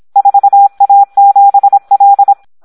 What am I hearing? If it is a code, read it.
Morse code, heard as a keyed tone, about 26 wpm, 800 Hz. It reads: 4A7L
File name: Audio practica CW / SD 1 26